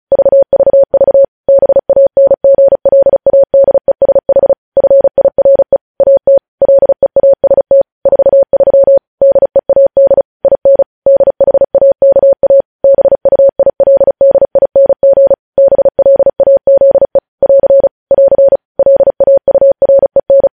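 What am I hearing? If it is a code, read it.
Morse code, heard as a keyed tone, about 36 wpm, 550 Hz. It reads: VVV BANGLADESH FIRE AT LEAST 43 DEAD IN DHAKA BUILDING BLAZE <AR> <AR> LAUREN
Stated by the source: News headlines sent in Morse at speed 35.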